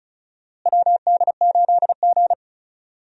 Morse code, heard as a keyed tone, about 34 wpm, 700 Hz. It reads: WD8G
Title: USA Call Signs - 4 Characters - Repeated 2x - 35wpm